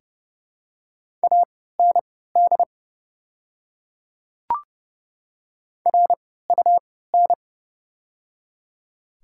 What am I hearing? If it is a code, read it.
Morse code, heard as a keyed tone, about 30 wpm, spaced as for 17 wpm, 700 Hz. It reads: AND RUN
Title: 3x Character Spacing - Top 500 Words - 30wpm